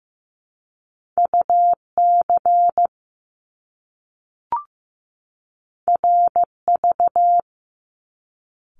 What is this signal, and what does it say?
Morse code, heard as a keyed tone, about 15 wpm, 700 Hz. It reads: UC RV
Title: Top Two Letter Combinations in Words 15wpm